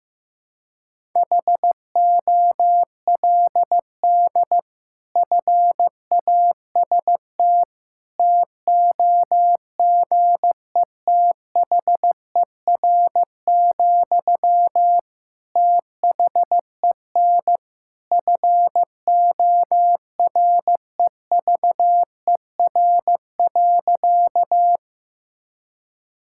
Morse code, heard as a keyed tone, about 15 wpm, 700 Hz. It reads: HOLD FAST TOGETHER, THEN FOREVER.
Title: Sentences from Top 1600 Words 15wpm